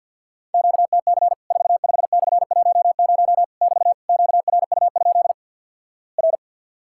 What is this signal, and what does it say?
Morse code, heard as a keyed tone, about 50 wpm, 700 Hz. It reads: QTC 45/10 <BT> QRU? R
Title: Worked All Europe DX Contest - QTC Exchange - 50wpm